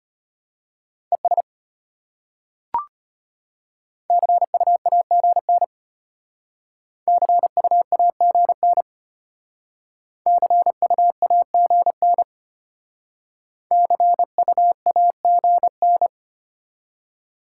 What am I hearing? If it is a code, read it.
Morse code, heard as a keyed tone, about 31 wpm, 700 Hz. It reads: ES CUAGN CUAGN CUAGN CUAGN
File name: practice / 25wpm / Speed-Racing Top 20 QSO Elements - 25wpm